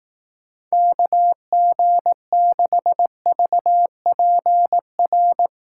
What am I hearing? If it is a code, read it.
Morse code, heard as a keyed tone, about 18 wpm, 700 Hz. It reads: KG6VPR
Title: Speed Racing - USA 2x3 Call Signs 15wpm